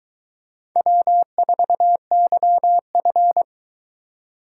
Morse code, heard as a keyed tone, about 20 wpm, 700 Hz. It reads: W4YF